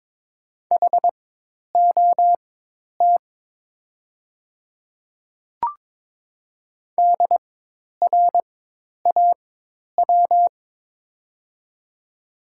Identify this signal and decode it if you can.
Morse code, heard as a keyed tone, about 22 wpm, spaced as for 10 wpm, 700 Hz. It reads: HOT DRAW
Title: 4x Character Spacing - Top 500 Words - 22wpm